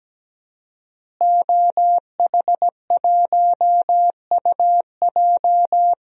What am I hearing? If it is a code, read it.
Morse code, heard as a keyed tone, about 17 wpm, 700 Hz. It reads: OH1UJ
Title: DXCC-100 Call Signs - 5 Characters - Repeated 3x - 17wpm